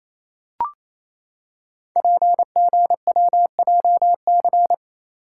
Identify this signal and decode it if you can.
Morse code, heard as a keyed tone, about 29 wpm, 700 Hz. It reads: PGWJC
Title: Code Groups - 5 Letters - 28wpm